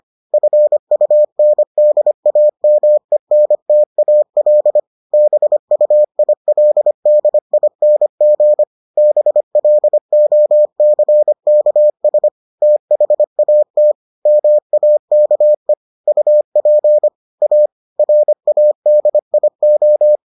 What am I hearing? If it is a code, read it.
Morse code, heard as a keyed tone, about 25 wpm, 600 Hz. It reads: FUNDAMENTAL BUILDING BLOCKS THAT MAKE UP A RADIO